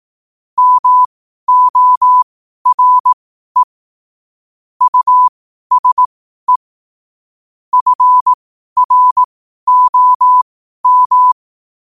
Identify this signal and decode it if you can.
Morse code, heard as a keyed tone, about 18 wpm, spaced as for 12 wpm, 1000 Hz. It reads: MORE USE FROM
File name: qotdfiles / 12wpm